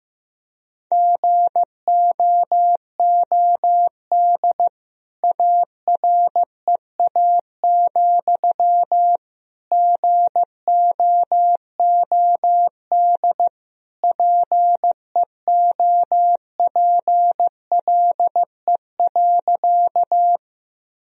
Morse code, heard as a keyed tone, about 15 wpm, 700 Hz. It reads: GOOD AREA, GOOD PEOPLE.
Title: Sentences from Top 1000 Words - Review Part 2 - 15wpm